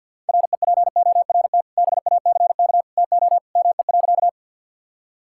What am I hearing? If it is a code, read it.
Morse code, heard as a keyed tone, about 47 wpm, 700 Hz. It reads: REPORT BACK TO ME.
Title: Sentences from Top 1000 Words - Review Part 6 - 50wpm